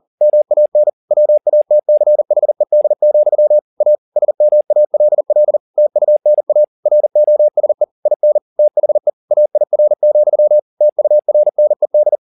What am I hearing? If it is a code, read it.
Morse code, heard as a keyed tone, about 40 wpm, 600 Hz. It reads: MAN WATCHED, A SMALL TUNA ROSE IN THE AIR, TURNED